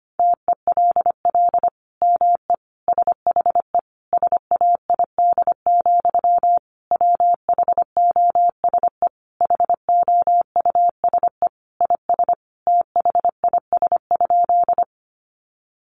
Morse code, heard as a keyed tone, about 25 wpm, 700 Hz. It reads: TELL ME SHE SAID, WHOSE HOUSE IS THIS?
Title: Sentences from Top 1200 Words 25wpm